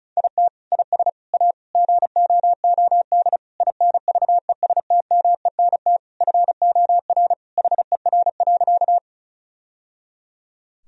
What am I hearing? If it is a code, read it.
Morse code, heard as a keyed tone, about 34 wpm, 700 Hz. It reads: IT IS A GOOD INVESTMENT FOR HER.
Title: Sentences from Top 1000 Words - Review Part 4 - 35wpm